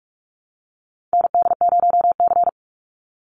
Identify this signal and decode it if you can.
Morse code, heard as a keyed tone, about 44 wpm, 690 Hz. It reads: ND0C
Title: NA SOTA Chaser Callsigns 45wpm